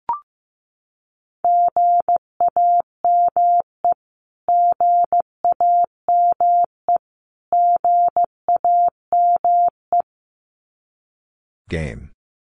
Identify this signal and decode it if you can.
Morse code, heard as a keyed tone, about 15 wpm, 700 Hz. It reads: GAME GAME GAME
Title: Top 500 Words Repeated 15wpm